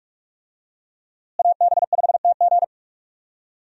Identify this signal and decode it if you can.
Morse code, heard as a keyed tone, about 44 wpm, 700 Hz. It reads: AB5TG
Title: USA Call Signs - 5 Characters - Repeated 3x - 45wpm